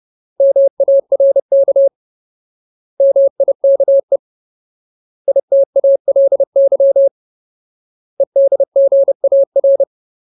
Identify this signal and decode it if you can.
Morse code, heard as a keyed tone, about 30 wpm, 550 Hz. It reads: MARK MIKE ITALY EDGAR
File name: Ultimate Word List / 30WPM Ultimate